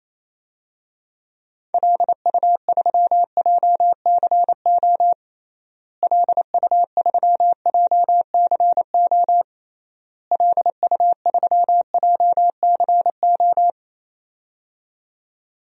Morse code, heard as a keyed tone, about 28 wpm, 700 Hz. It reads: LU3JCO LU3JCO LU3JCO
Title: DXCC-100 Call Signs - 6 Characters - Repeated 3x - 28wpm